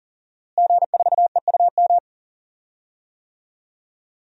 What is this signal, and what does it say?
Morse code, heard as a keyed tone, about 40 wpm, 700 Hz. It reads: G4EUM
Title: DXCC-100 Call Signs - 5 Characters - Repeated 3x - 40wpm